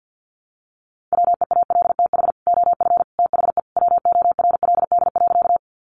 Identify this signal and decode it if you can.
Morse code, heard as a keyed tone, about 50 wpm, 700 Hz. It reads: WEALTH OF THE WORLD.
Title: Sentences from Top 1200 Words 50wpm